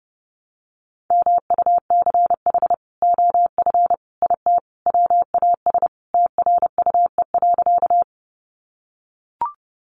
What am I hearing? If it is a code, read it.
Morse code, heard as a keyed tone, about 30 wpm, 700 Hz. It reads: MUCH OF IT WAS TRUE.
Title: Sentences from Top 400 Words 30wpm